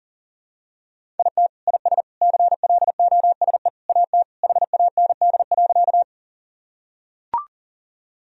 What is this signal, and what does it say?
Morse code, heard as a keyed tone, about 40 wpm, 700 Hz. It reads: IT IS CLOSE AT HAND.
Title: Sentences from Top 1000 Words - Review Part 1 - 40wpm